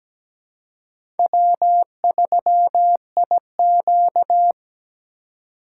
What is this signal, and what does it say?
Morse code, heard as a keyed tone, about 17 wpm, 700 Hz. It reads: W3IQ
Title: USA Call Signs - 4 Characters - 3x Character Spacing - 17wpm